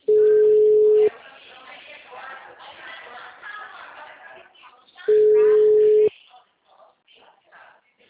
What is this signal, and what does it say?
Ringback tone (Europe, 425 Hz), with speech.
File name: out- -2013- - - .56966